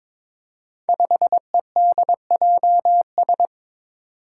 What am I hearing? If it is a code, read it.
Morse code, heard as a keyed tone, about 22 wpm, 700 Hz. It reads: 5EDJS
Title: Five Challenging Letter-Number Combinations 22wpm